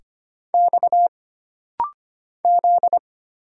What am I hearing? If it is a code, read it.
Morse code, heard as a keyed tone, about 25 wpm, 700 Hz. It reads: X Z
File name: Single Character (Letter:Number:Punctuation:Prosign) - Rapid-Fire - 25wpm